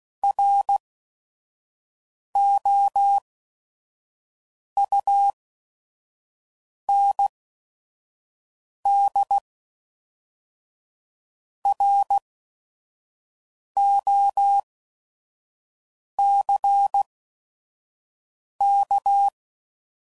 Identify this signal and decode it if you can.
Morse code, heard as a keyed tone, about 16 wpm, spaced as for 5 wpm, 800 Hz. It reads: ROUNDROCK
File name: CW Code Groups @5WPM / fromSM's / Cities